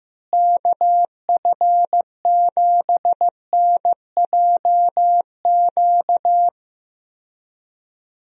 Morse code, heard as a keyed tone, about 15 wpm, 700 Hz. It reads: KF7NJQ
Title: USA Call Signs - 6 Characters - Repeated 3x - 15wpm